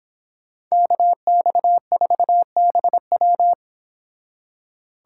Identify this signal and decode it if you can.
Morse code, heard as a keyed tone, about 28 wpm, 700 Hz. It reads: KX4BW